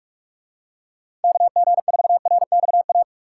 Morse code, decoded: KG4RXA

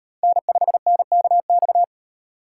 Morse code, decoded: N5NKX

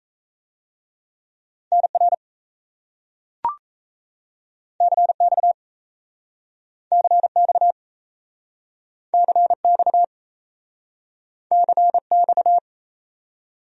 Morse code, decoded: NR CX CX CX CX

